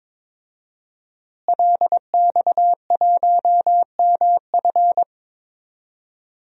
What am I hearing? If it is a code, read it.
Morse code, heard as a keyed tone, about 22 wpm, 700 Hz. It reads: LX1MF
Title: DXCC-100 Call Signs - 5 Characters - Repeated 2x - 22wpm